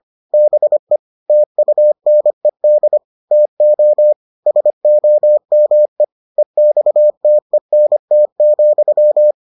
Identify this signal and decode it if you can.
Morse code, heard as a keyed tone, about 25 wpm, 600 Hz. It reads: BE TUNED TO SOME EXTENT,